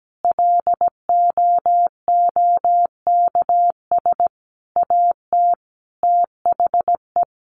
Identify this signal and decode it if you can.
Morse code, heard as a keyed tone, about 17 wpm, 700 Hz. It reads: LOOKS AT THE
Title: Additional Easy Phrases 17wpm